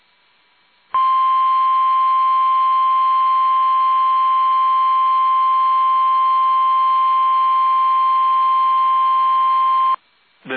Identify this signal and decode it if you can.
Emergency Alert System (SAME) tones (the 1050 Hz attention tone). It is heard from 0.9 to 10.0 s.
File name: eas / KUSO / KUSO